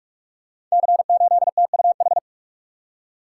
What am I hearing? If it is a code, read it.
Morse code, heard as a keyed tone, about 44 wpm, 690 Hz. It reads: C8TUH
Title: Code Groups - 5 Letter:Numbers - 45wpm